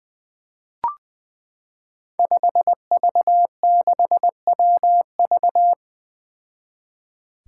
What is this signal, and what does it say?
Morse code, heard as a keyed tone, about 20 wpm, 700 Hz. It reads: 5V6WV
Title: Five Challenging Letter-Number Combinations 20wpm